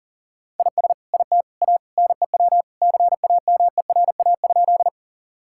Morse code, heard as a keyed tone, about 37 wpm, 700 Hz. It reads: IS IT A NEW CAMERA?
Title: Sentences from Top 1000 Words - Review Part 5 - 40wpm